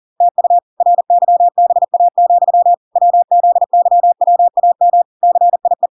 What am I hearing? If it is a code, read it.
Morse code, heard as a keyed tone, about 40 wpm, 700 Hz. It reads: TU RYBA, WZYWAM CIE